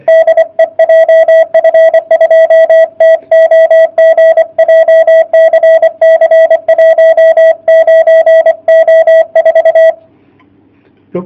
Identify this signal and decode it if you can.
Morse code, heard as a keyed tone, about 23 wpm, 660 Hz. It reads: DE JF2TOG JCC 19O4
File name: jh8cbh / 2019cw / sp31- w